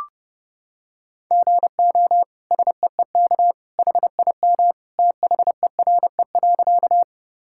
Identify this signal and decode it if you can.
Morse code, heard as a keyed tone, about 30 wpm, 700 Hz. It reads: GO SEEK HIM THERE.
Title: Sentences from Top 1600 Words 30wpm